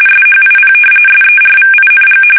RTTY, about 45.45 baud. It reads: SHIP SERVING
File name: fsk850 / battleshipnj